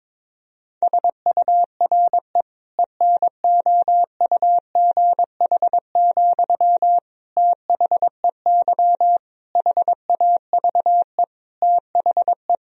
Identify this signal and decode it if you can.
Morse code, heard as a keyed tone, about 22 wpm, 700 Hz. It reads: SURE ENOUGH, THEY HAVE THE
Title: Sentences from Top 1000 Words - Review Part 2 - 22wpm